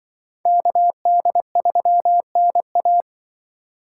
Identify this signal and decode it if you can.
Morse code, heard as a keyed tone, about 24 wpm, 700 Hz. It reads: KD3NA
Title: Speed Racing - USA 2x2 Call Signs 20wpm